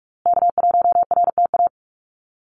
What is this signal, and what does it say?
Morse code, heard as a keyed tone, about 44 wpm, 700 Hz. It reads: K1RTA
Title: USA Call Signs Repeated 2x 45wpm